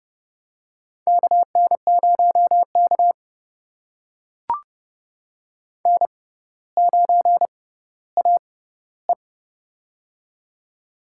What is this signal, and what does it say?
Morse code, heard as a keyed tone, about 30 wpm, spaced as for 15 wpm, 700 Hz. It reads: KN0K N9AE
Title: USA Call Signs - 4 Characters - 6x Character Spacing - 30wpm